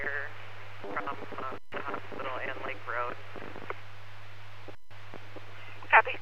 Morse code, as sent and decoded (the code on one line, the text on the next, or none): none